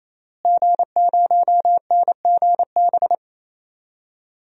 Morse code, heard as a keyed tone, about 27 wpm, 700 Hz. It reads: G0NGB
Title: DXCC-100 Call Signs - 5 Characters - Repeated 3x - 28wpm